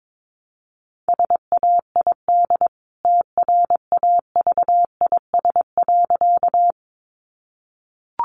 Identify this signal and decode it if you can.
Morse code, heard as a keyed tone, about 22 wpm, 700 Hz. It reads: SAID TRAVIS.
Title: A Sound Of Thunder 22wpm